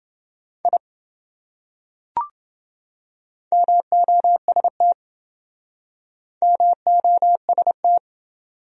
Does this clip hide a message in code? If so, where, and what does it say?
Yes: Morse code from 0.6 to 8.0 s, heard as a keyed tone, about 26 wpm, 700 Hz. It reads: I MOST MOST